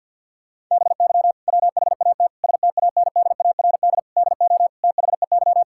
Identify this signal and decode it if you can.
Morse code, heard as a keyed tone, about 50 wpm, 700 Hz. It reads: BY WHAT STANDARD DO THEY
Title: Sentences from Top 1000 Words - Review Part 6 - 50wpm